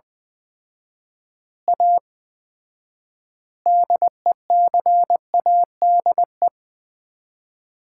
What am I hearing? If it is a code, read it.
Morse code, heard as a keyed tone, about 20 wpm, 700 Hz. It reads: A DECADE
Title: 4x Word-Spacing - Sets of 5 Words - 20wpm